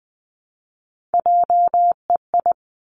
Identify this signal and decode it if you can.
Morse code, heard as a keyed tone, about 20 wpm, 700 Hz. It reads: JEI
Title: Three Challenging Letter-Number Combinations 20wpm